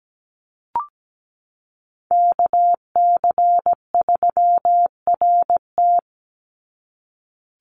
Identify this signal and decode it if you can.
Morse code, heard as a keyed tone, about 17 wpm, 700 Hz. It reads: KC3RT
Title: NA SOTA Chaser Callsigns 17wpm